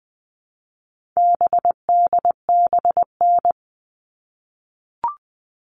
Morse code, heard as a keyed tone, about 20 wpm, 700 Hz. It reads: BDBN
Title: All Cut Numbers - 4-Digit 20wpm